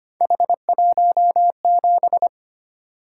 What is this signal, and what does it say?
Morse code, heard as a keyed tone, about 25 wpm, 700 Hz. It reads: H17
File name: Three Letter-Number Combinations - 25wpm